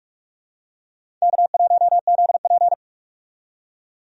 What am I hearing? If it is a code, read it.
Morse code, heard as a keyed tone, about 46 wpm, 700 Hz. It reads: K1ZP
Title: USA Call Signs - 4 Characters - 45wpm